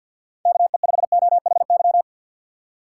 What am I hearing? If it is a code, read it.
Morse code, heard as a keyed tone, about 52 wpm, 700 Hz. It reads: KE5OHY
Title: USA Call Signs - 6 Characters - 6x Character Spacing - 50wpm